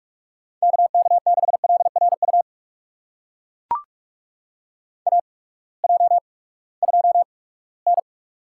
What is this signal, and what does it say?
Morse code, heard as a keyed tone, about 41 wpm, spaced as for 22 wpm, 700 Hz. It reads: KK6LRU AJ2N
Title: USA Call Signs - 8x Character Spacing - 45wpm